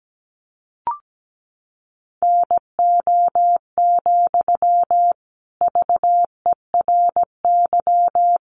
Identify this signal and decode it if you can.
Morse code, heard as a keyed tone, about 17 wpm, 700 Hz. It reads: NO, VERY
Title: Sentences from Top 1500 Words 17wpm